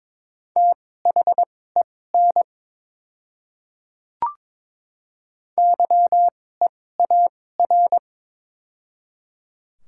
Morse code, heard as a keyed tone, about 22 wpm, spaced as for 16 wpm, 700 Hz. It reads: THEN YEAR
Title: 2x Character Spacing - Top 300 Words - 22wpm